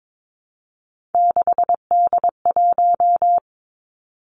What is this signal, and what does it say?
Morse code, heard as a keyed tone, about 22 wpm, 700 Hz. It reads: 6D1
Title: Three Challenging Letter-Number Combinations 22wpm